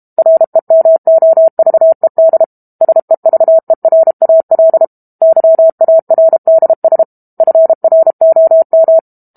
Morse code, heard as a keyed tone, about 32 wpm, 650 Hz. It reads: REMOVED SEVERAL YARDS FROM
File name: Book Two / 06-Chapter 6 - 32 WPM